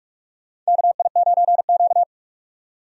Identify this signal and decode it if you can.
Morse code, heard as a keyed tone, about 43 wpm, 700 Hz. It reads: KI9Q
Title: USA Call Signs - 4 Characters - 45wpm